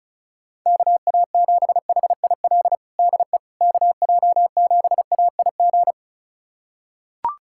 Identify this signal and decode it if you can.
Morse code, heard as a keyed tone, about 34 wpm, 700 Hz. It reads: KA7HIL DE KJ7AIG
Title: USA Callsign DE Callsign - 6 Characters - 35wpm